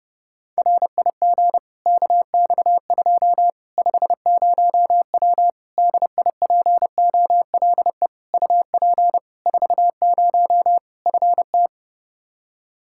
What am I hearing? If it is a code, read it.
Morse code, heard as a keyed tone, about 32 wpm, 700 Hz. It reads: RIG KX2 50W DIPOLE UP 40 FT
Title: Practice QSO - Exchange 2 - Info - 30wpm